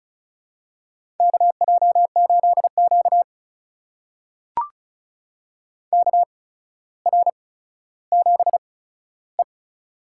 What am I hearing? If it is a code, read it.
Morse code, heard as a keyed tone, about 34 wpm, spaced as for 15 wpm, 700 Hz. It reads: KJ8Q KR7E
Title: USA Call Signs - 4 Characters - 8x Character Spacing - 35wpm